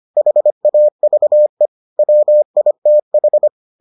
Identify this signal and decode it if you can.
Morse code, heard as a keyed tone, about 25 wpm, 600 Hz. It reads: HAVE WITH